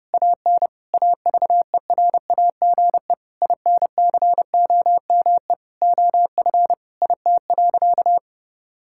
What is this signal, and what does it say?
Morse code, heard as a keyed tone, about 30 wpm, 700 Hz. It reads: AN AVERAGE INCOME OF IT.
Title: Sentences from Top 1000 Words - Review Part 6 - 30wpm